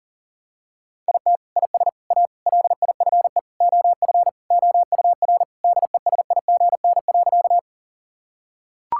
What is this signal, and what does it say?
Morse code, heard as a keyed tone, about 40 wpm, 700 Hz. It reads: IT IS A LIFE OF OUR DESIGN.
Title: Sentences from Top 1000 Words - Review Part 5 - 40wpm